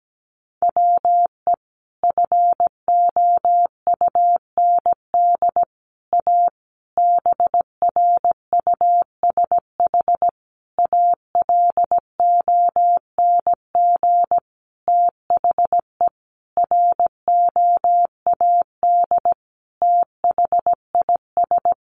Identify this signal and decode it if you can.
Morse code, heard as a keyed tone, about 17 wpm, 700 Hz. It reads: WE FOUND A BRUSH ALONG THE ROAD THIS
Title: Sentences from Top 1100 Words 17wpm